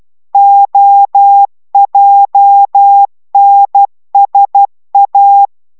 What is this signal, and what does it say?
Morse code, heard as a keyed tone, about 12 wpm, 800 Hz. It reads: OJNSA